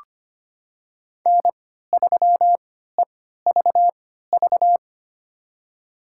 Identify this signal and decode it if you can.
Morse code, heard as a keyed tone, about 25 wpm, spaced as for 14 wpm, 700 Hz. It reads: N3EVV
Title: USA Call Signs - 5 Characters - 3x Character Spacing - 25wpm